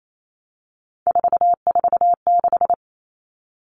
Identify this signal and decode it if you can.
Morse code, heard as a keyed tone, about 28 wpm, 700 Hz. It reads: 446
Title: Numbers - 3-Digit 28wpm